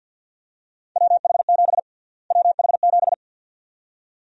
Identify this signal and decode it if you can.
Morse code, heard as a keyed tone, about 51 wpm, 700 Hz. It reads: WH7 WH7
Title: USA Call Sign Prefixes - Repeated 2x - 50wpm